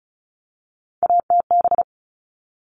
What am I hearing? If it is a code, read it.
Morse code, heard as a keyed tone, about 34 wpm, 700 Hz. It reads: ATB
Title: All Cut Numbers - 3-Digit 35wpm